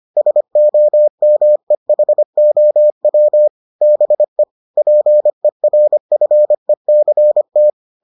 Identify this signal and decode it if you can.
Morse code, heard as a keyed tone, about 25 wpm, 600 Hz. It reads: SOMEHOW BE PERFECT